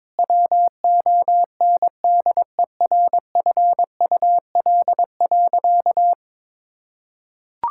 Morse code, heard as a keyed tone, about 22 wpm, 700 Hz. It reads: WONDERFUL.
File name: practice / 22wpm / Sentences with 3-Word Phrases - 22wpm